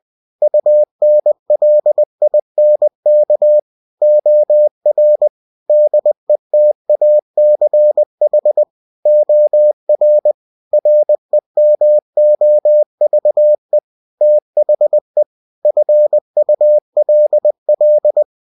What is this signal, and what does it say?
Morse code, heard as a keyed tone, about 20 wpm, 600 Hz. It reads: UNLINK OR DETACH OR REMOVE THE FULL